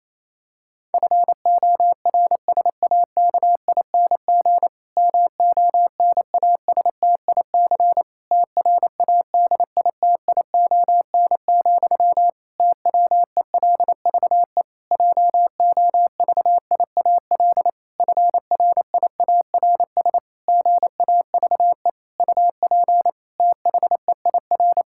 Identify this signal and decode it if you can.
Morse code, heard as a keyed tone, about 27 wpm, 700 Hz. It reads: FORSAKING MONASTIC TRADITION, TWELVE JOVIAL FRIARS GAVE UP THEIR
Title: Pangram Challenge v2 - 28wpm